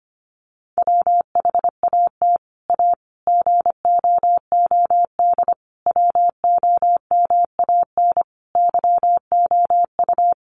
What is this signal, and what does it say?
Morse code, heard as a keyed tone, about 25 wpm, 700 Hz. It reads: WHAT A GOOD WOMAN YOU
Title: Sentences from Top 1100 Words 25wpm